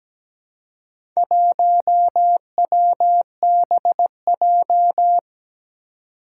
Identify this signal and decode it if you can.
Morse code, heard as a keyed tone, about 17 wpm, 700 Hz. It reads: 1WBJ
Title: Four Challenging Letter-Number Combinations 17wpm